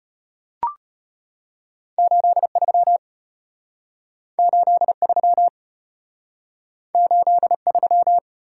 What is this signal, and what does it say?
Morse code, heard as a keyed tone, about 34 wpm, 700 Hz. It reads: 83 83 83